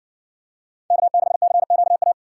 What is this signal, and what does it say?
Morse code, heard as a keyed tone, about 60 wpm, 700 Hz. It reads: K6CQA